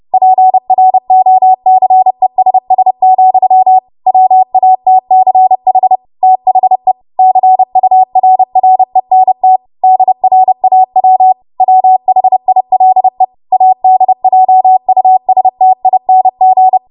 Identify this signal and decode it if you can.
Morse code, heard as a keyed tone, about 30 wpm, 750 Hz. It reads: PROCESS, WATCH THE CURRENT DRAW WHILE ADJUSTING
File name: Archive / 30 WPM / 30WPM